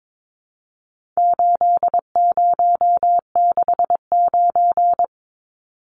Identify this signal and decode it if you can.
Morse code, heard as a keyed tone, about 22 wpm, 700 Hz. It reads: 8069
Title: Numbers - 4-Digit 22wpm